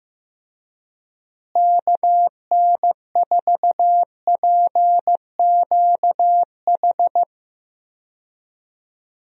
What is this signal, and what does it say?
Morse code, heard as a keyed tone, about 15 wpm, 700 Hz. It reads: KN4PQH
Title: USA Call Signs - 6 Characters - Repeated 2x - 15wpm